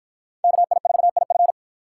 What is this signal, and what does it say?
Morse code, heard as a keyed tone, about 53 wpm, 700 Hz. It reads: KI4IF